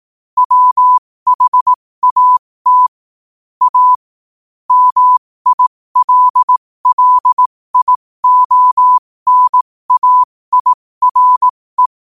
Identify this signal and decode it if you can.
Morse code, heard as a keyed tone, about 18 wpm, spaced as for 15 wpm, 1000 Hz. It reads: WHAT A MILLIONAIRE
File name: qotdfiles / 15wpm